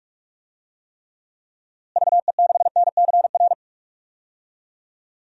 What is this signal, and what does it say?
Morse code, heard as a keyed tone, about 46 wpm, 700 Hz. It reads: VE6NCR
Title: DXCC-100 Call Signs - 6 Characters - Repeated 3x - 45wpm